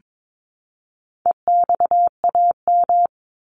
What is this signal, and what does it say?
Morse code, heard as a keyed tone, about 22 wpm, 700 Hz. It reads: EXAM
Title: Top 501-750 Words 22wpm